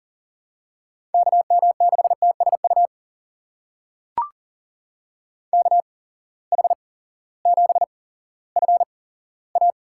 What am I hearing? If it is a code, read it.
Morse code, heard as a keyed tone, about 37 wpm, spaced as for 18 wpm, 700 Hz. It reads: KM6TSU KH7FA
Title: USA Call Signs - 8x Character Spacing - 40wpm